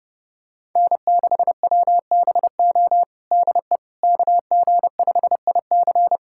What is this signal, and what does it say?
Morse code, heard as a keyed tone, about 30 wpm, 700 Hz. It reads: N6WBO DE KG5IC
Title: USA Callsign DE Callsign - 5 Characters - 30wpm